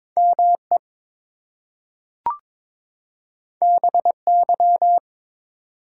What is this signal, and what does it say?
Morse code, heard as a keyed tone, about 22 wpm, 700 Hz. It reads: ME BY
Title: Two Letter Words - 22wpm